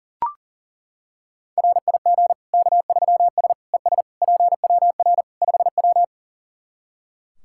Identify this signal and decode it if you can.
Morse code, heard as a keyed tone, about 40 wpm, 700 Hz. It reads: RIG K3S ES PWR 5W
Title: K1USN SST QSO Practice 40wpm